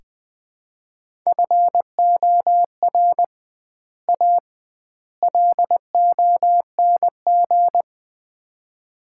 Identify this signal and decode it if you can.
Morse code, heard as a keyed tone, about 20 wpm, 700 Hz. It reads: FOR A LONG
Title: 2x Word-Spacing - Sets of 3 Words - 20wpm